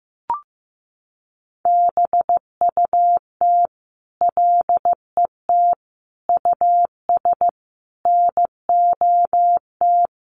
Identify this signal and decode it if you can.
Morse code, heard as a keyed tone, about 15 wpm, 700 Hz. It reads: BUT LET US NOT
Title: Sentences from Top 1500 Words 15wpm